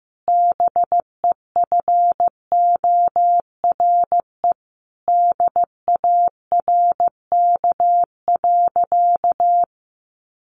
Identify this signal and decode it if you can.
Morse code, heard as a keyed tone, about 15 wpm, 700 Hz. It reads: BEFORE DARK.
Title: Sentences from Top 1100 Words 15wpm